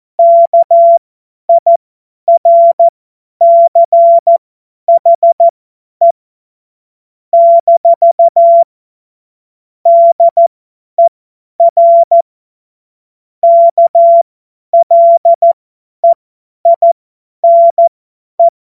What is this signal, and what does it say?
Morse code, heard as a keyed tone, about 14 wpm, spaced as for 10 wpm, 680 Hz. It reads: KIRCHE - DER KLEINE
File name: Morsekurs / Witze / Witz 02. November 2021, 50 F70